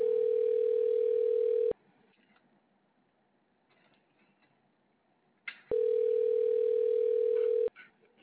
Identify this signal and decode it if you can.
Ringback tone (North America), 440 + 480 Hz.